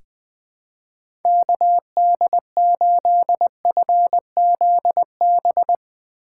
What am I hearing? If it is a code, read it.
Morse code, heard as a keyed tone, about 20 wpm, 700 Hz. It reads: KD8FZB
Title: USA Call Signs - 6 Characters - 3x Character Spacing - 20wpm